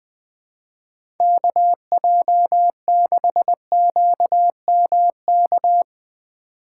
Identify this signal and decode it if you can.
Morse code, heard as a keyed tone, about 21 wpm, 700 Hz. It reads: KJ6QMK
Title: Speed Racing - USA 2x3 Call Signs 20wpm